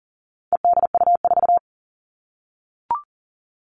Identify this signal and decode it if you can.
Morse code, heard as a keyed tone, about 40 wpm, 700 Hz. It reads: EDU4
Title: All Cut Numbers - 4-Digit 40wpm